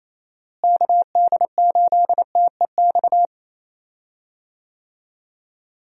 Morse code, heard as a keyed tone, about 27 wpm, 700 Hz. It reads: KD8TEX
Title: USA Call Signs - 6 Characters - Repeated 3x - 28wpm